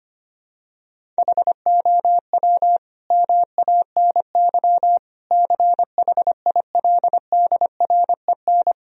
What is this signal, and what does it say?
Morse code, heard as a keyed tone, about 25 wpm, 700 Hz. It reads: HOW MANY CHILDREN
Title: Sentences from Top 1000 Words - Review Part 1 - 25wpm